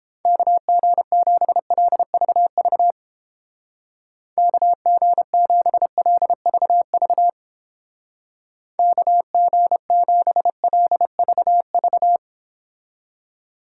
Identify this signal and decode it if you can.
Morse code, heard as a keyed tone, about 31 wpm, 700 Hz. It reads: KG7LVV KG7LVV KG7LVV